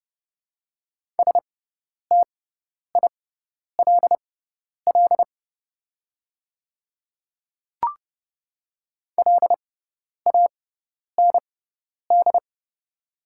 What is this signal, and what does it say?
Morse code, heard as a keyed tone, about 30 wpm, spaced as for 10 wpm, 690 Hz. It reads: STILL LAND
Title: 6x Character Spacing - Top 300 Words - 30wpm